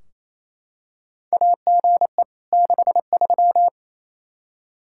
Morse code, heard as a keyed tone, about 27 wpm, 700 Hz. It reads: AGE 63
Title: Practice QSO - Exchange 3 - More Info - 28wpm